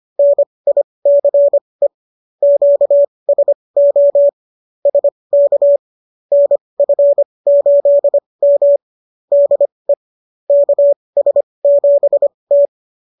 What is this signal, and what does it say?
Morse code, heard as a keyed tone, about 25 wpm, spaced as for 20 wpm, 570 Hz. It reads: NICE QSO SK NF8M DE KS7T
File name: qso221 20